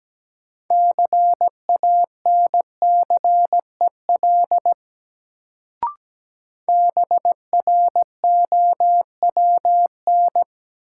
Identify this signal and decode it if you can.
Morse code, heard as a keyed tone, about 17 wpm, 700 Hz. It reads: CANCEL BROWN
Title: Sentences from Top 1200 Words - New Word Drill - 17wpm